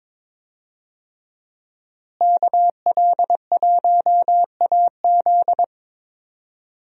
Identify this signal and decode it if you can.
Morse code, heard as a keyed tone, about 22 wpm, 700 Hz. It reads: KL1AZ
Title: USA Call Signs - 5 Characters - Repeated 3x - 22wpm